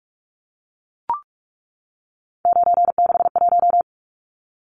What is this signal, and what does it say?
Morse code, heard as a keyed tone, about 44 wpm, 700 Hz. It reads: 961
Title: Numbers - 3-Digit 45wpm